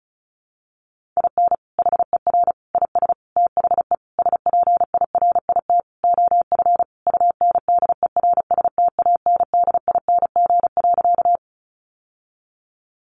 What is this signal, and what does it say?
Morse code, heard as a keyed tone, about 34 wpm, 700 Hz. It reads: IN HER IS THE SPIRIT OF UNDERSTANDING.
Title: Sentences from Top 1400 Words 35wpm